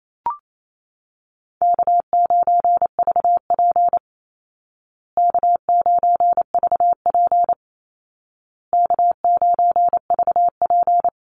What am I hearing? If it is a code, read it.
Morse code, heard as a keyed tone, about 28 wpm, 700 Hz. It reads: K9VP K9VP K9VP
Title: Call Signs Repeated 2x - 28wpm